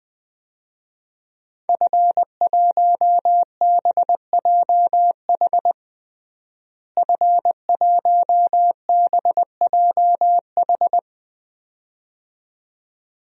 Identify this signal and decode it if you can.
Morse code, heard as a keyed tone, about 20 wpm, 700 Hz. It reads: F1BJH F1BJH
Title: DXCC-100 Call Signs - 5 Characters - Repeated 2x - 20wpm